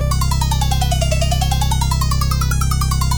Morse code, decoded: none